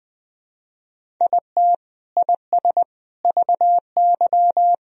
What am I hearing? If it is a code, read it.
Morse code, heard as a keyed tone, about 21 wpm, 700 Hz. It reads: IT IS VY
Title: Sentences from Top 500 Words and Abbreviations 20wpm